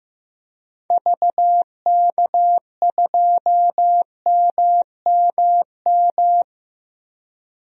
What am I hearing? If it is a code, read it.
Morse code, heard as a keyed tone, about 15 wpm, 700 Hz. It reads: VK2MMM
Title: DXCC-100 Call Signs - 6 Characters - 15wpm